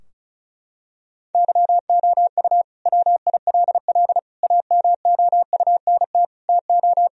Morse code, heard as a keyed tone, about 36 wpm, 700 Hz. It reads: YOU WILL AMOUNT TO
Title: Sentences from Top 1000 Words - Review Part 4 - 35wpm